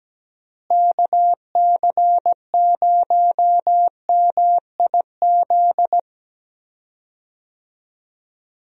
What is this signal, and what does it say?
Morse code, heard as a keyed tone, about 17 wpm, 700 Hz. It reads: KC0MIZ
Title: USA Call Signs - 6 Characters - Repeated 2x - 17wpm